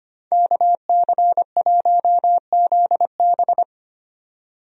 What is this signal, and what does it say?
Morse code, heard as a keyed tone, about 25 wpm, 700 Hz. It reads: KC1ZB
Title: USA Call Signs - 5 Characters - 25wpm